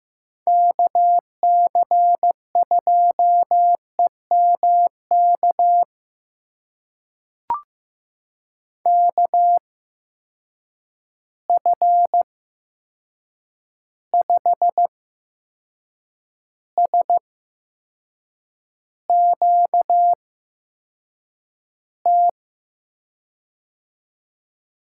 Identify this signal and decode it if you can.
Morse code, heard as a keyed tone, about 15 wpm, spaced as for 6 wpm, 700 Hz. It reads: KC2EMK KF5SQT
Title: USA Call Signs - 6 Characters - 8x Character Spacing - 15wpm